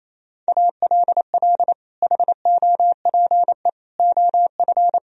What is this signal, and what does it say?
Morse code, heard as a keyed tone, about 28 wpm, 700 Hz. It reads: ALL HOPE OF
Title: Sentences from Top 1000 Words - Review Part 4 - 28wpm